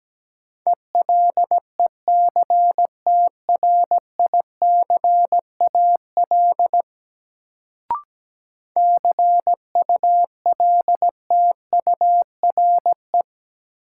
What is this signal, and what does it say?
Morse code, heard as a keyed tone, about 17 wpm, 700 Hz. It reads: ELECTRICAL CULTURE
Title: Sentences from Top 1000 Words - Review Part 6 - New Word Drill - 17wpm